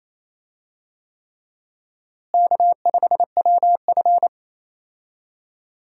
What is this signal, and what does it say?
Morse code, heard as a keyed tone, about 28 wpm, 700 Hz. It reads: K5WF
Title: USA Call Signs - 4 Characters - Repeated 3x - 28wpm